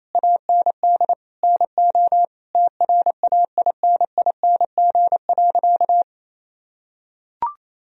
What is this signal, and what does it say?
Morse code, heard as a keyed tone, about 28 wpm, 700 Hz. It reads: AND NO TRAINING.
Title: Sentences from Top 1000 Words - Review Part 4 - 28wpm